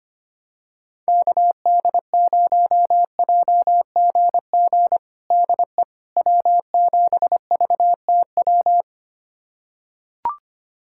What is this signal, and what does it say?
Morse code, heard as a keyed tone, about 25 wpm, 700 Hz. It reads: KD0JGG DE W7VTW
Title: USA Callsign DE Callsign 25wpm